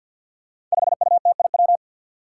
Morse code, heard as a keyed tone, about 51 wpm, 700 Hz. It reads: 5UTIW
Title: Five Challenging Letter-Number Combinations 50wpm